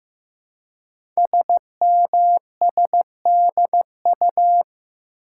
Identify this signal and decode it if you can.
Morse code, heard as a keyed tone, about 15 wpm, 700 Hz. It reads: SMSDU
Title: Code Groups - 5 Letters - 15wpm